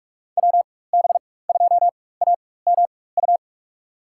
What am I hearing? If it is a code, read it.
Morse code, heard as a keyed tone, about 46 wpm, spaced as for 21 wpm, 700 Hz. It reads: WB2AMU
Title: USA Call Signs - 6 Characters - 4x Character Spacing - 45wpm